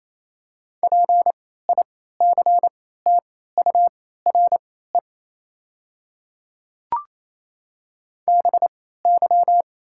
Morse code, heard as a keyed tone, about 28 wpm, spaced as for 16 wpm, 700 Hz. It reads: PICTURE BY
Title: 3x Character Spacing - Top 200 Words - 28wpm